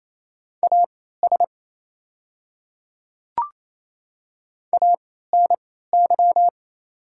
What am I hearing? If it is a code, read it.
Morse code, heard as a keyed tone, about 28 wpm, spaced as for 16 wpm, 700 Hz. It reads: AS ANY
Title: 3x Character Spacing - Top 200 Words - 28wpm